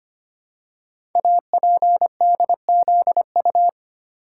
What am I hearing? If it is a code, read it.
Morse code, heard as a keyed tone, about 25 wpm, 700 Hz. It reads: APDZU
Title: Code Groups - 5 Letters - 25wpm